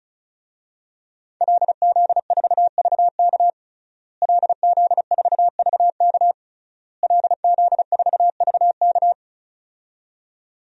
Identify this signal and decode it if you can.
Morse code, heard as a keyed tone, about 34 wpm, 700 Hz. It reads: LZ4VK LZ4VK LZ4VK
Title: DXCC-100 Call Signs - 5 Characters - Repeated 3x - 35wpm